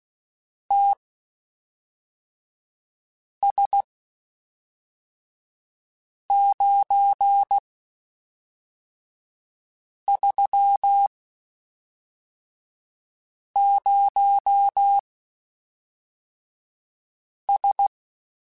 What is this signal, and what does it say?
Morse code, heard as a keyed tone, about 16 wpm, spaced as for 3 wpm, 800 Hz. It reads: TS930S
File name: (spaced) / QSOs Spaced-mp3s / kc7haa @5 sp